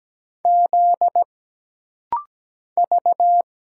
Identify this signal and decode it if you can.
Morse code, heard as a keyed tone, about 17 wpm, 700 Hz. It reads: Z V
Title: Single Letters - Rapid-Fire 17wpm